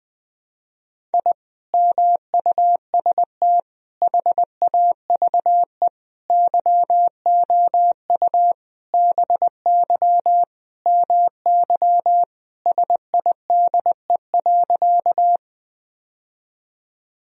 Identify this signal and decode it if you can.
Morse code, heard as a keyed tone, about 20 wpm, 700 Hz. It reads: I MUST HAVE YOU BY MY SIDE.
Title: Sentences from Top 1000 Words - Review Part 1 - 20wpm